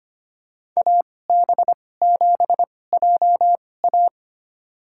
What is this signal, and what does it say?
Morse code, heard as a keyed tone, about 25 wpm, spaced as for 18 wpm, 700 Hz. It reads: AB7JA
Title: USA Call Signs - 5 Characters - 2x Character Spacing - 25wpm